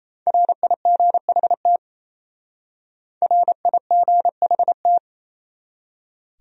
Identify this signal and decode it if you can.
Morse code, heard as a keyed tone, about 34 wpm, 700 Hz. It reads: RIGHT RIGHT
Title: Speed-Racing Top 200 Words 28wpm